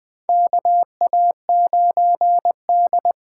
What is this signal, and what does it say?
Morse code, heard as a keyed tone, about 20 wpm, 700 Hz. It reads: KA9D
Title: USA Call Signs - 4 Characters - 3x Character Spacing - 20wpm